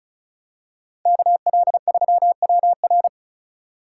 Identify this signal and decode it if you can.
Morse code, heard as a keyed tone, about 36 wpm, 700 Hz. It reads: KL3WR
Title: USA Call Signs - 4x Character Spacing - 35wpm